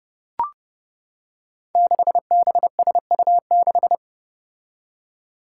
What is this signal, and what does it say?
Morse code, heard as a keyed tone, about 30 wpm, 700 Hz. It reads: 6BSU6
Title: Five Challenging Letter-Number Combinations 30wpm